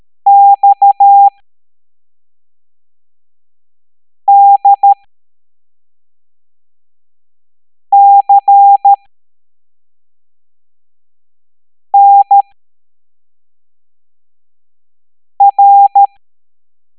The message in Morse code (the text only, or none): XDCNR